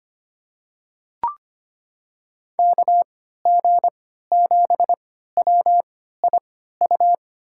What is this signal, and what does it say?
Morse code, heard as a keyed tone, about 25 wpm, spaced as for 14 wpm, 700 Hz. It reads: KG7WIU
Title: USA Call Signs - 6 Characters - 3x Character Spacing - 25wpm